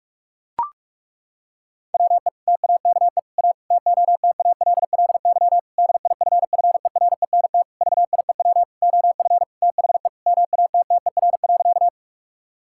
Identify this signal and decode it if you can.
Morse code, heard as a keyed tone, about 43 wpm, 700 Hz. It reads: WE TAKE A TOTALLY DIFFERENT VIEW OF THE MATTER.
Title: Sentences from Top 1600 Words 45wpm